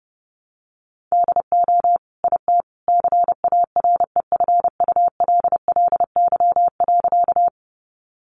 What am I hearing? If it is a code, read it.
Morse code, heard as a keyed tone, about 30 wpm, 700 Hz. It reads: DO IT CAREFULLY.
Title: Sentences from Top 1300 Words 30wpm